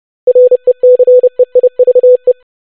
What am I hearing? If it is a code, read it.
Morse code, heard as a keyed tone, about 30 wpm, 500 Hz. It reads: RECEIVE
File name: basic500-25wpm